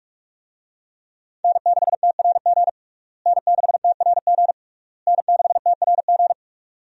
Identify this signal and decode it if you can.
Morse code, heard as a keyed tone, about 42 wpm, 700 Hz. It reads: N6TRG N6TRG N6TRG
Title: USA Call Signs - 5 Characters - Repeated 3x - 45wpm